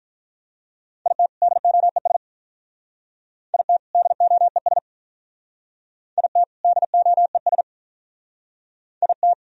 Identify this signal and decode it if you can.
Morse code, heard as a keyed tone, about 46 wpm, 700 Hz. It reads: IT DOES IT DOES IT DOES IT